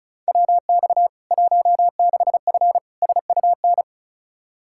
Morse code, heard as a keyed tone, about 35 wpm, 700 Hz. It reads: WX 16F SUN
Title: K1USN SST QSO Practice 35wpm